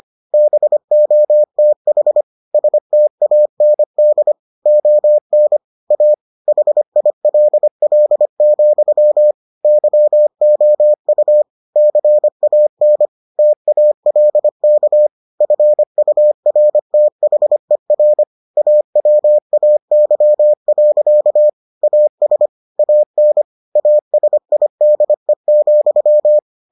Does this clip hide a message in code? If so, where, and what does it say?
Yes: Morse code from 0.3 to 26.4 s, heard as a keyed tone, about 25 wpm, 600 Hz. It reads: BOTH STAND ON A HILL, YOU CAN TALK FURTHER AWAY. AS AN ASIDE,